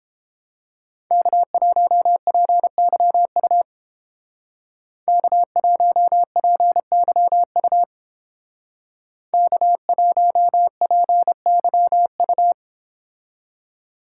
Morse code, heard as a keyed tone, about 32 wpm, 700 Hz. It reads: K1PYU K1PYU K1PYU